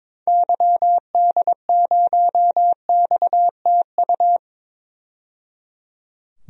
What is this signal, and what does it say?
Morse code, heard as a keyed tone, about 22 wpm, 700 Hz. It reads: YD0XTU
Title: DXCC-100 Call Signs - 6 Characters - 22wpm